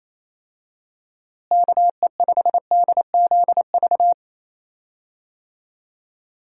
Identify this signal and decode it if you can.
Morse code, heard as a keyed tone, about 28 wpm, 700 Hz. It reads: KE5DZV
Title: USA Call Signs - 6 Characters - Repeated 2x - 28wpm